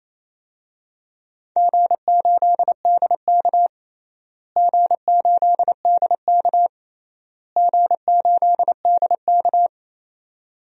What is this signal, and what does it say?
Morse code, heard as a keyed tone, about 28 wpm, 700 Hz. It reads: G8DK G8DK G8DK
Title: DXCC-100 Call Signs - 4 Characters - Repeated 3x - 28wpm